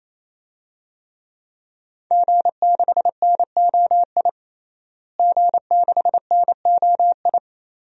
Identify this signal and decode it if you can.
Morse code, heard as a keyed tone, about 28 wpm, 700 Hz. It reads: G6NOI G6NOI
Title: DXCC-100 Call Signs - 5 Characters - Repeated 3x - 28wpm